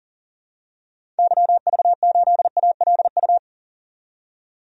Morse code, heard as a keyed tone, about 40 wpm, 700 Hz. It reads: YV8ALU
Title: DXCC-100 Call Signs - 6 Characters - 40wpm